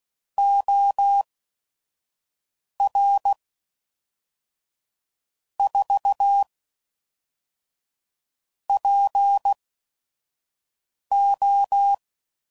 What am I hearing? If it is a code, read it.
Morse code, heard as a keyed tone, about 16 wpm, spaced as for 5 wpm, 800 Hz. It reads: OR 4 PO
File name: CW Code Groups @5WPM / Technical Subjects / AND Gate